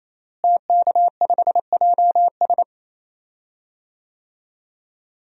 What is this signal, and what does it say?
Morse code, heard as a keyed tone, about 28 wpm, 700 Hz. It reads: TK5JS
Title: DXCC-100 Call Signs - 5 Characters - Repeated 3x - 28wpm